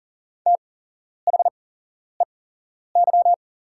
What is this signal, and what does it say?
Morse code, heard as a keyed tone, about 40 wpm, spaced as for 11 wpm, 700 Hz. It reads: THEY
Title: 8x Character Spacing - Top 100 Words - 40wpm